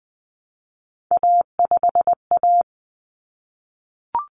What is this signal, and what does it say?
Morse code, heard as a keyed tone, about 20 wpm, 700 Hz. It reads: A5A
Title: Common Cut Numbers - 3-Digit 20wpm